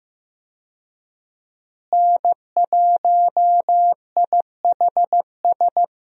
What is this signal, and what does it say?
Morse code, heard as a keyed tone, about 15 wpm, 700 Hz. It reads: N1IHS
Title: USA Call Signs - 5 Characters - Repeated 3x - 15wpm